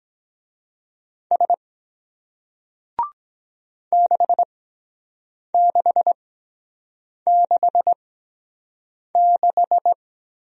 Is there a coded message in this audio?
Yes: Morse code from 1.3 to 9.9 s, heard as a keyed tone, about 22 wpm, spaced as for 7 wpm, 700 Hz. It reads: S 6666